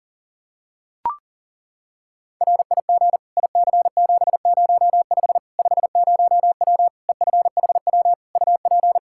Morse code, heard as a keyed tone, about 40 wpm, 700 Hz. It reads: RIG IC705 50W EFHW UP
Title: Practice QSO - Exchange 2 - Info - 40wpm